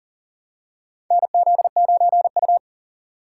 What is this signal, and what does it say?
Morse code, heard as a keyed tone, about 40 wpm, 700 Hz. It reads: NZ9U
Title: USA Call Signs - 4 Characters - 2x Character Spacing - 40wpm